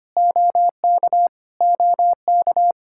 Morse code, heard as a keyed tone, about 25 wpm, 700 Hz. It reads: OK OK
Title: Practice QSO - Exchange 1 - Beginning - 25wpm